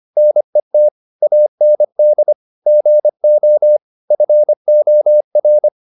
Morse code, heard as a keyed tone, about 25 wpm, 600 Hz. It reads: NET AND GO FOR
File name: the old man and the sea part 1 25wpm